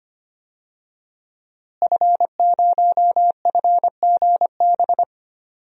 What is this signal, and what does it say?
Morse code, heard as a keyed tone, about 25 wpm, 700 Hz. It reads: F0FGB
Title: DXCC-100 Call Signs - 5 Characters - Repeated 3x - 25wpm